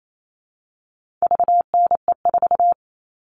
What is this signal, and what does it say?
Morse code, heard as a keyed tone, about 28 wpm, 700 Hz. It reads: VNE4
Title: All Cut Numbers - 4-Digit 28wpm